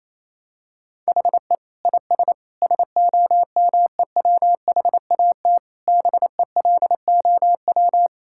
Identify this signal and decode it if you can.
Morse code, heard as a keyed tone, about 28 wpm, 700 Hz. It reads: HE IS SOMEWHAT BELOW
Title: Sentences from Top 1600 Words 28wpm